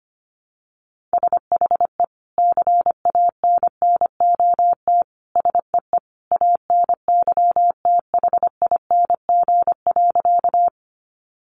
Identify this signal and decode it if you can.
Morse code, heard as a keyed tone, about 25 wpm, 700 Hz. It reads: SHE CANNOT SEE ANYTHING.
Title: Additional Easy Phrases 25wpm